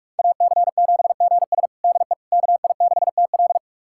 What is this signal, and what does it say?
Morse code, heard as a keyed tone, about 42 wpm, 700 Hz. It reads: AC7GS DE KI6TL
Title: USA Callsign DE Callsign - 5 Characters - 45wpm